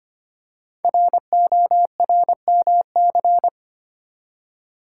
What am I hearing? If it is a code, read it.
Morse code, heard as a keyed tone, about 25 wpm, 700 Hz. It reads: RORMC
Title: Code Groups - 5 Letters - 25wpm